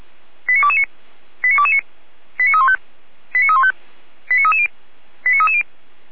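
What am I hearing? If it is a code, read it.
Five-tone selective call (ZVEI), heard as quick runs of five tones, tones of about 70 ms. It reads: 89229 89229 89315 89315 89330 89330
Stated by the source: ZVEI five-tone selective call received by SDR, from audio.f85360.feuerwehr-orlamuende.org fire brigade server.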